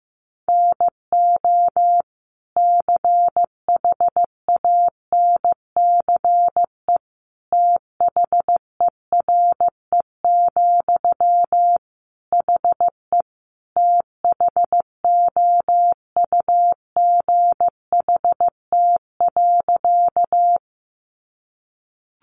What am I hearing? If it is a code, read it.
Morse code, heard as a keyed tone, about 15 wpm, 700 Hz. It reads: NO CHANCE THERE, HE THOUGHT.
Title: Sentences from Top 800 Words 15wpm